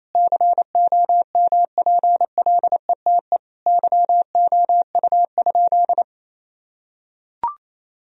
Morse code, heard as a keyed tone, about 28 wpm, 700 Hz. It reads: COMPLETE YOU?
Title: Sentences from Top 1000 Words - Review Part 2 - 28wpm